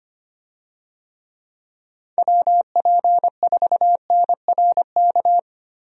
Morse code, heard as a keyed tone, about 25 wpm, 700 Hz. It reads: WP4NRK
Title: USA Call Signs - 6 Characters - Repeated 3x - 25wpm